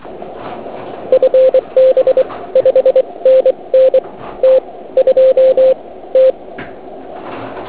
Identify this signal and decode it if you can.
Morse code, heard as a keyed tone, about 24 wpm, spaced as for 19 wpm, 520 Hz. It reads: FB 5NN T2 T
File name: Archiv / CUC106 / cuc106 PILEUP